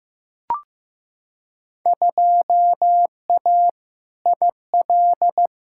Morse code, heard as a keyed tone, about 15 wpm, 700 Hz. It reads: 2A IL
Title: Field Day Exchanges - 15wpm